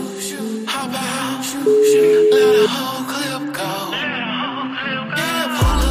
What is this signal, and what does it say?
Ringback tone (Europe, 425 Hz), with noise.